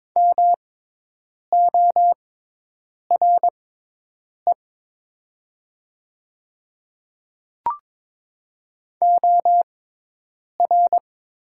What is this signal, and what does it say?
Morse code, heard as a keyed tone, about 22 wpm, spaced as for 8 wpm, 700 Hz. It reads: MORE OR
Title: 6x Character Spacing - Top 100 Words - 22wpm